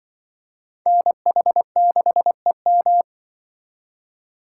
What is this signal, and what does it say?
Morse code, heard as a keyed tone, about 25 wpm, 700 Hz. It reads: NH6EM
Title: Speed Racing - USA Call Signs 20wpm